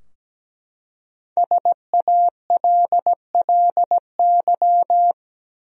Morse code, heard as a keyed tone, about 17 wpm, 700 Hz. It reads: SALLY
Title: USA Common Names - Max 5 Letters - 17wpm